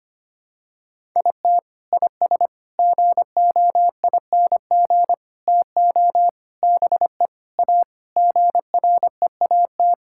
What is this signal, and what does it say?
Morse code, heard as a keyed tone, about 25 wpm, 700 Hz. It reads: IT IS GOING TO BE A GREAT
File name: practice / 25wpm / Sentences with 3-Word Phrases - 25wpm